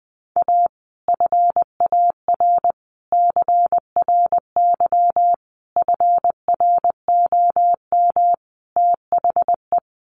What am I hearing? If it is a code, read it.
Morse code, heard as a keyed tone, about 19 wpm, 700 Hz. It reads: A FAR CRY FROM THE
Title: Sets of 5 Words Encore 20wpm